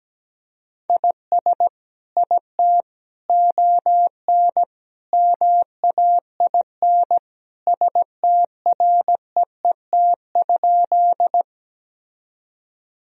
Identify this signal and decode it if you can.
Morse code, heard as a keyed tone, about 17 wpm, 700 Hz. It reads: IS IT ON MAIN STREET?
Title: Sentences from Top 1000 Words - Review Part 2 - 17wpm